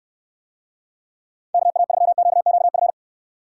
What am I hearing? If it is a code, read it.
Morse code, heard as a keyed tone, about 69 wpm, 700 Hz. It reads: KA3QQF